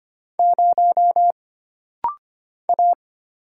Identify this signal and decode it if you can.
Morse code, heard as a keyed tone, about 25 wpm, 700 Hz. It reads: 0 A
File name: Single Character (Letter:Number:Punctuation:Prosign) - Rapid-Fire - 25wpm